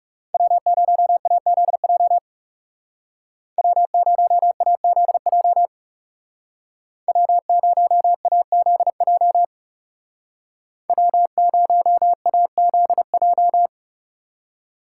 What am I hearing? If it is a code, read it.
Morse code, heard as a keyed tone, about 35 wpm, 700 Hz. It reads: W0AZJ W0AZJ W0AZJ W0AZJ
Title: Speed Racing - USA 1x3 Call Signs 30wpm